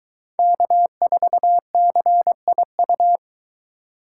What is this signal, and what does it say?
Morse code, heard as a keyed tone, about 22 wpm, 700 Hz. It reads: K4CIU